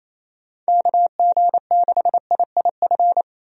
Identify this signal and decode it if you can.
Morse code, heard as a keyed tone, about 28 wpm, 700 Hz. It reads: KG6IIF
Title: USA Call Signs - 6 Characters - 28wpm